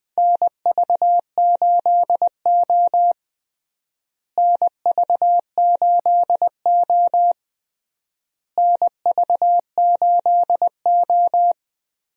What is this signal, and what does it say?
Morse code, heard as a keyed tone, about 21 wpm, 700 Hz. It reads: NV8O NV8O NV8O
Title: USA Call Signs - 4 Characters - Repeated 3x - 20wpm